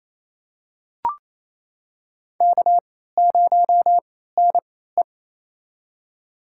Morse code, heard as a keyed tone, about 28 wpm, spaced as for 16 wpm, 700 Hz. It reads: K0NE
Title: USA Call Signs - 4 Characters - 3x Character Spacing - 28wpm